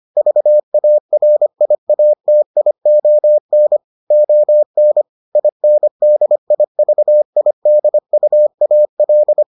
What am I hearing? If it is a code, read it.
Morse code, heard as a keyed tone, about 25 wpm, 600 Hz. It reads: VARIATION ON INDIVIDUAL